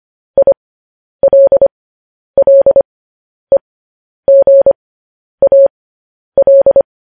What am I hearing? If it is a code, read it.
Morse code, heard as a keyed tone, about 25 wpm, spaced as for 10 wpm, 550 Hz. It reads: ILLEGAL
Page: news-headlines-in-morse-code-at-10-wpm